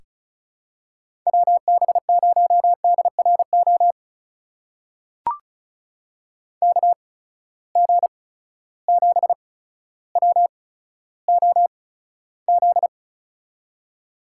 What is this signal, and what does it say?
Morse code, heard as a keyed tone, about 36 wpm, spaced as for 15 wpm, 700 Hz. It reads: WB0DRO KG7WOZ
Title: USA Call Signs - 6 Characters - 8x Character Spacing - 35wpm